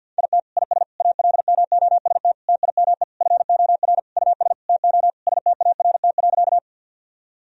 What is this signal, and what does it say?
Morse code, heard as a keyed tone, about 50 wpm, 700 Hz. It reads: IT IS ALMOST TIME FOR US TO START.
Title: Sentences from Top 1000 Words - Review Part 4 - 50wpm